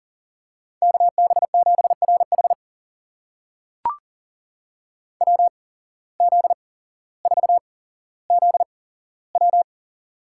Morse code, heard as a keyed tone, about 37 wpm, spaced as for 17 wpm, 700 Hz. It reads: KB7RH WZ4ZW
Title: USA Call Signs - 5 Characters - 8x Character Spacing - 40wpm